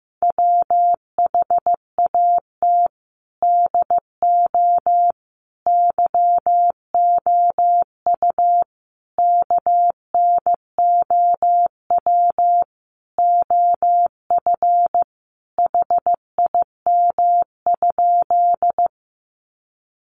Morse code, heard as a keyed tone, about 15 wpm, 700 Hz. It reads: WHAT DO YOU KNOW OF HIM?
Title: Sentences from Top 100 Words 15wpm